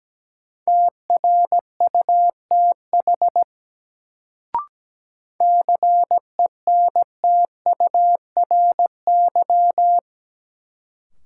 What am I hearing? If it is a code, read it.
Morse code, heard as a keyed tone, about 17 wpm, 700 Hz. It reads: TRUTH CENTURY
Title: Sets of 5 Words - New Word Drill - 17wpm